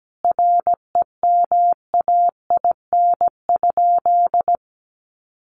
Morse code, heard as a keyed tone, about 17 wpm, 700 Hz. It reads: REMAIN?
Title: Sentences from Top 1200 Words 17wpm